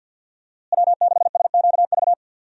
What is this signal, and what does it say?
Morse code, heard as a keyed tone, about 52 wpm, 700 Hz. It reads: W6SQV
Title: USA Call Signs - 5 Characters - Repeated 3x - 50wpm